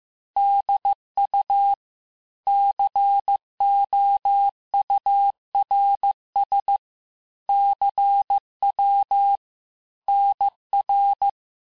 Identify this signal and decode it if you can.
Morse code, heard as a keyed tone, about 15 wpm, 800 Hz. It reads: DU COURS CW NR
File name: cours-du-lundi files / cours-1585-texte-chiffre-vitesse-14mpm